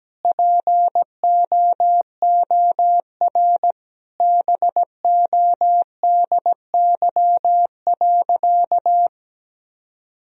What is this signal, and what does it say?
Morse code, heard as a keyed tone, about 17 wpm, 700 Hz. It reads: POOR BODY.
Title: Sentences from Top 1000 Words - Review Part 2 - 17wpm